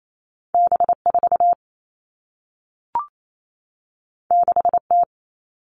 Morse code, heard as a keyed tone, about 28 wpm, 700 Hz. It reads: B4 6T
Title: All Cut Numbers - 2-Digit 28wpm